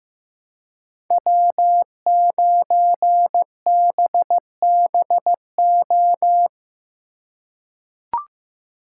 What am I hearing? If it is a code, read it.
Morse code, heard as a keyed tone, about 15 wpm, 700 Hz. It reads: W9BBO
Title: USA Call Signs - 5 Characters - 3x Character Spacing - 15wpm